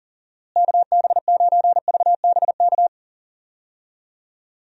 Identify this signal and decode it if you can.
Morse code, heard as a keyed tone, about 40 wpm, 700 Hz. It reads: KB9VBK
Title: USA Call Signs - 6 Characters - Repeated 2x - 40wpm